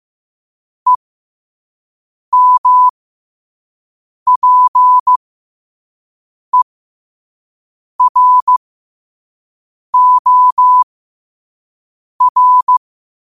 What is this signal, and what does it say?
Morse code, heard as a keyed tone, about 15 wpm, spaced as for 5 wpm, 1000 Hz. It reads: EMPEROR
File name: qotdfiles / 5wpm